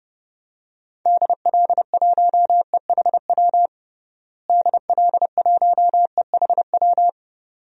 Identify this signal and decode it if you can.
Morse code, heard as a keyed tone, about 32 wpm, 700 Hz. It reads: DL1EHW DL1EHW
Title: DXCC-100 Call Signs - 6 Characters - Repeated 2x - 30wpm